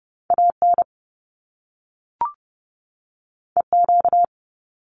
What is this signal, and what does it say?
Morse code, heard as a keyed tone, about 30 wpm, 700 Hz. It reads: AN EQ
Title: Top Two Letter Combinations in Words 30wpm